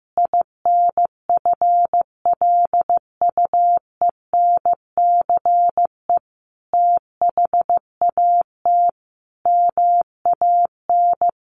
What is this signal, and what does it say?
Morse code, heard as a keyed tone, about 14 wpm, 700 Hz. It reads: INFLUENCE THAT MAN
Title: Sentences from Top 1100 Words 15wpm